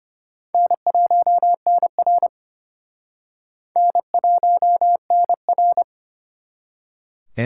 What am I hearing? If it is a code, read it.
Morse code, heard as a keyed tone, about 32 wpm, 700 Hz. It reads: N1NR N1NR